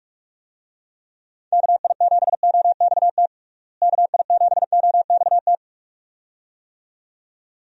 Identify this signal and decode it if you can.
Morse code, heard as a keyed tone, about 46 wpm, 700 Hz. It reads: KI7OXT KI7OXT
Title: USA Call Signs - 6 Characters - Repeated 2x - 45wpm